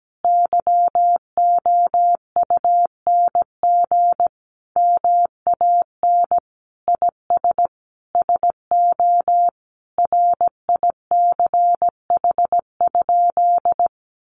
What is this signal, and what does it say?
Morse code, heard as a keyed tone, about 17 wpm, 700 Hz. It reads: YOUNG MAN IS SO RICH?
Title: Sentences from Top 1100 Words 17wpm